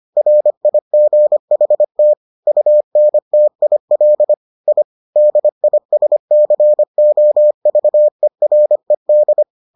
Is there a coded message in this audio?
Yes: Morse code from 0.2 to 9.4 s, heard as a keyed tone, about 25 wpm, 600 Hz. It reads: RIGHT UNTIL I DISCOVERED